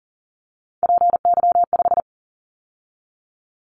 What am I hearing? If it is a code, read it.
Morse code, heard as a keyed tone, about 41 wpm, 700 Hz. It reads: PY5
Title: Callsign Prefixes for Top 50 DXCC - 40wpm